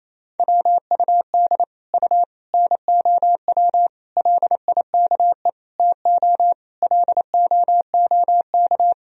Morse code, heard as a keyed tone, about 27 wpm, 700 Hz. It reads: WUD U NOW LIKE TO LOOK
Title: Sentences from Top 200 Words and Abbreviations 28wpm